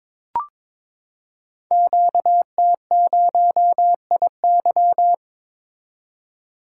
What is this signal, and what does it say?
Morse code, heard as a keyed tone, about 22 wpm, 700 Hz. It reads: QT0IY
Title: Code Groups - 5 Letter:Numbers - 22wpm